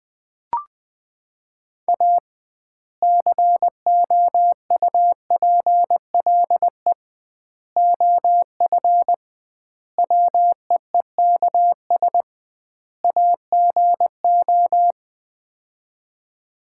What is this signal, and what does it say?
Morse code, heard as a keyed tone, about 20 wpm, 700 Hz. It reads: A COUPLE OF WEEKS AGO
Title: 2x Word-Spacing - Sets of 5 Words - 20wpm